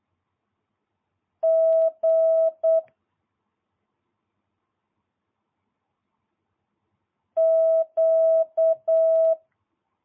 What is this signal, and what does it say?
Morse code, heard as a keyed tone, about 8 wpm, 650 Hz. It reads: G Q